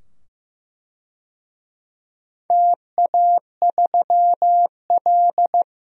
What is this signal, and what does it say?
Morse code, heard as a keyed tone, about 15 wpm, 700 Hz. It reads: TA3L
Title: DXCC-100 Call Signs - 4 Characters - 15wpm